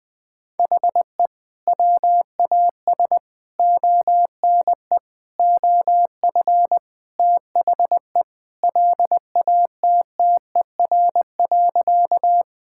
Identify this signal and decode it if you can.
Morse code, heard as a keyed tone, about 20 wpm, 700 Hz. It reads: HE WAS ONE OF THE LATTER.
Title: Sentences from Top 1600 Words 20wpm